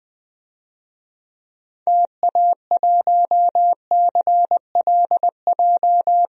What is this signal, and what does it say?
Morse code, heard as a keyed tone, about 20 wpm, 700 Hz. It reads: TA1CLJ
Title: DXCC-100 Call Signs - 6 Characters - 20wpm